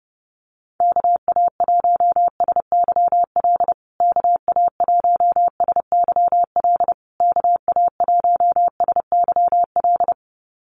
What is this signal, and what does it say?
Morse code, heard as a keyed tone, about 30 wpm, 700 Hz. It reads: KA1SYL KA1SYL KA1SYL
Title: USA Call Signs Repeated 3x 30wpm